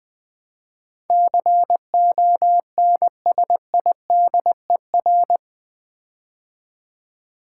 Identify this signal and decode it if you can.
Morse code, heard as a keyed tone, about 20 wpm, 700 Hz. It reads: CONSIDER
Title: Build A Word - 4 Letters At A Time - 12-Letter Words - 20wpm